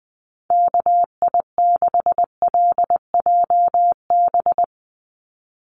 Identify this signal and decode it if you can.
Morse code, heard as a keyed tone, about 20 wpm, 700 Hz. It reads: KI6LJB
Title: USA Call Signs 20wpm